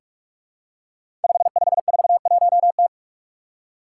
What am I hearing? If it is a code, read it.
Morse code, heard as a keyed tone, about 44 wpm, 700 Hz. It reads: 5541T
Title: Five Challenging Letter-Number Combinations 45wpm